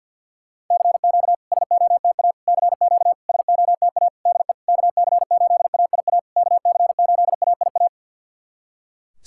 Morse code, heard as a keyed tone, about 50 wpm, 700 Hz. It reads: CQ SOTA CQ SOTA DE KC8AIA KC8AIA
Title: SOTA QSO Practice 50wpm